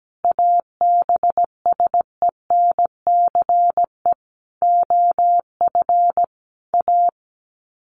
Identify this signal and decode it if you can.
Morse code, heard as a keyed tone, about 17 wpm, 700 Hz. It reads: ABSENCE OF A
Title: Sets of 5 Words Encore 17wpm